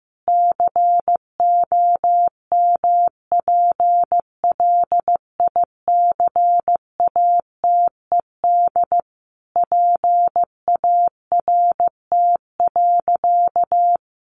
Morse code, heard as a keyed tone, about 15 wpm, 700 Hz. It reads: COMPLICATED PART.
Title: Sentences from Top 1200 Words 15wpm